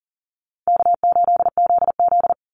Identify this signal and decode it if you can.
Morse code, heard as a keyed tone, about 40 wpm, 700 Hz. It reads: K8ZZ
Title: Easy Call Signs - 40wpm